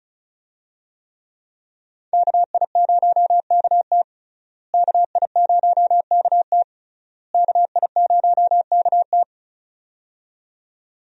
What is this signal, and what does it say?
Morse code, heard as a keyed tone, about 36 wpm, 700 Hz. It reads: KI0KT KI0KT KI0KT
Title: USA Call Signs - 5 Characters - Repeated 3x - 35wpm